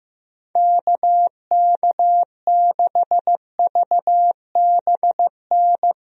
Morse code, heard as a keyed tone, about 15 wpm, 700 Hz. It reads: KK6VBN
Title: USA Call Signs - 6 Characters - Repeated 3x - 15wpm